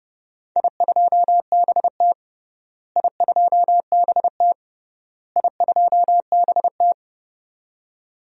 Morse code, heard as a keyed tone, about 32 wpm, 700 Hz. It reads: I2BT I2BT I2BT
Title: DXCC-100 Call Signs - 4 Characters - Repeated 3x - 30wpm